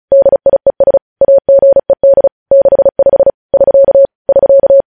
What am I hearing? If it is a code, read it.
Morse code, heard as a keyed tone, about 36 wpm, 550 Hz. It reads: DIES AGED 65 <SK> <SK>
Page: news headlines in morse code at 35 wpm